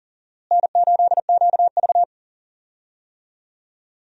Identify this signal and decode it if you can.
Morse code, heard as a keyed tone, about 40 wpm, 700 Hz. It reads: N8QV
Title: USA Call Signs - 4 Characters - Repeated 2x - 40wpm